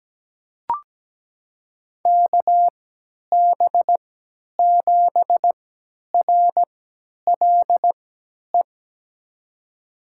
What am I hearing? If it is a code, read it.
Morse code, heard as a keyed tone, about 17 wpm, spaced as for 10 wpm, 700 Hz. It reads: KB7RLE
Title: USA Call Signs - 6 Characters - 3x Character Spacing - 17wpm